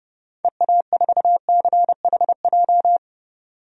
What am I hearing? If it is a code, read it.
Morse code, heard as a keyed tone, about 30 wpm, 700 Hz. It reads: EA4CHJ
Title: DXCC-100 Call Signs - 6 Characters - 30wpm